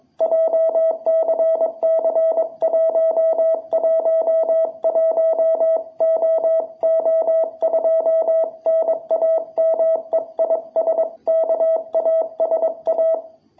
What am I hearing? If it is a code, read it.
Morse code, heard as a keyed tone, about 22 wpm, 660 Hz. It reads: JCC111OO2 NAME IS KASA
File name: jh8cbh / 2021cw